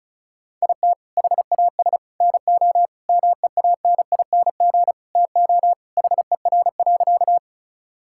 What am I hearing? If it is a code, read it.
Morse code, heard as a keyed tone, about 36 wpm, 700 Hz. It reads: IT HAS NO MEANING TO HER.
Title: Sentences from Top 1000 Words - Review Part 6 - 35wpm